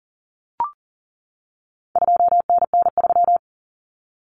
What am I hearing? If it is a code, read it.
Morse code, heard as a keyed tone, about 39 wpm, 700 Hz. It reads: 2NN3
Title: Common Cut Numbers - 4-Digit 40wpm